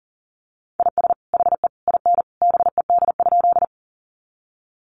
Morse code, heard as a keyed tone, about 40 wpm, 700 Hz. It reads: IS HE IN BED?
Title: Sentences from Top 500 Words 40wpm